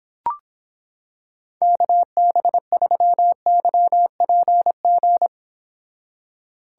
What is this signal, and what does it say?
Morse code, heard as a keyed tone, about 26 wpm, 700 Hz. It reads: KB3YPG